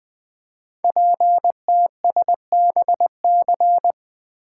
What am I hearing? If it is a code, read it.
Morse code, heard as a keyed tone, about 20 wpm, 700 Hz. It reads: PTSBC
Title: Code Groups - 5 Letters - 20wpm